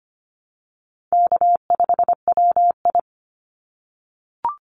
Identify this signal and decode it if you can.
Morse code, heard as a keyed tone, about 25 wpm, 700 Hz. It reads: K5WI
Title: NA SOTA Chaser Callsigns 25wpm